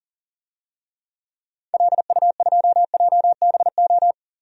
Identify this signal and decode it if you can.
Morse code, heard as a keyed tone, about 40 wpm, 700 Hz. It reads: LU2JBO
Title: DXCC-100 Call Signs - 6 Characters - 40wpm